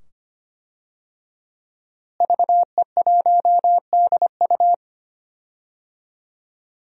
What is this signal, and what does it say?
Morse code, heard as a keyed tone, about 25 wpm, 700 Hz. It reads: VE1DU
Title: DXCC-100 Call Signs - 5 Characters - Repeated 2x - 25wpm